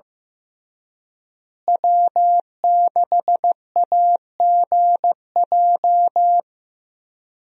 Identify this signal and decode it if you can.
Morse code, heard as a keyed tone, about 15 wpm, 700 Hz. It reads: W6AGJ
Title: USA Call Signs - 5 Characters - Repeated 3x - 15wpm